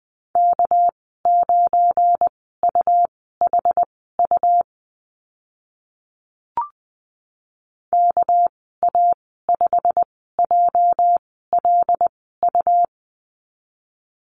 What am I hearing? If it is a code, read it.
Morse code, heard as a keyed tone, about 20 wpm, spaced as for 14 wpm, 700 Hz. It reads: K9UHU KA5JLU
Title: 219 Lesson 35g - Number 0 - Call Signs 20wpm